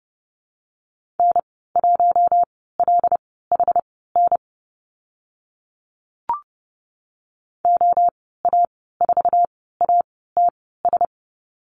Morse code, heard as a keyed tone, about 30 wpm, spaced as for 17 wpm, 700 Hz. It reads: N1LHN OA4ATS
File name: Lesson 14g - Number 2 - Call Signs 30wpm